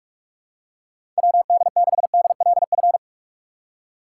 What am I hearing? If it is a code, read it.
Morse code, heard as a keyed tone, about 44 wpm, 700 Hz. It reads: W D 6 D L F
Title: USA Call Signs - 6 Characters - 6x Character Spacing - 45wpm